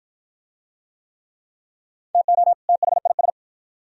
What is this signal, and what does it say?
Morse code, heard as a keyed tone, about 51 wpm, 700 Hz. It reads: TO THIS